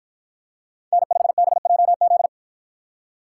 Morse code, heard as a keyed tone, about 48 wpm, 700 Hz. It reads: N5BJZ